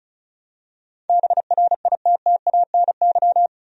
Morse code, heard as a keyed tone, about 34 wpm, 700 Hz. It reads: BRITTANY
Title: USA Common Names - 35wpm